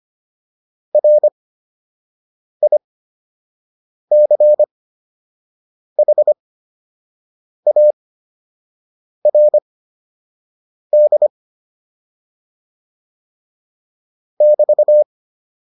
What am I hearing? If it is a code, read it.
Morse code, heard as a keyed tone, about 25 wpm, spaced as for 6 wpm, 600 Hz. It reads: RICHARD <BT>
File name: QSO 203 06